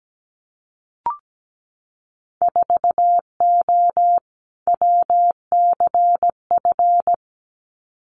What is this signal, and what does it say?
Morse code, heard as a keyed tone, about 17 wpm, 700 Hz. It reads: 4O WCF
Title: Winter Field Day Exchanges Encore 17wpm